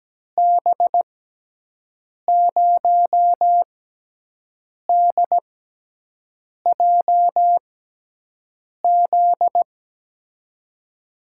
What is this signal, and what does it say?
Morse code, heard as a keyed tone, about 17 wpm, spaced as for 6 wpm, 700 Hz. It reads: B0DJZ
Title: USA Call Signs - 6 Characters - 6x Character Spacing - 17wpm